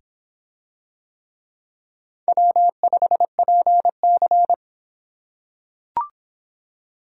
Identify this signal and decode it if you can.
Morse code, heard as a keyed tone, about 23 wpm, 700 Hz. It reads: W5PC